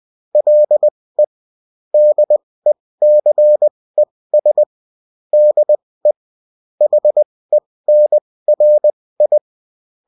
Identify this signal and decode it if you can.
Morse code, heard as a keyed tone, about 20 wpm, spaced as for 16 wpm, 600 Hz. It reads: LE DECES DE HENRI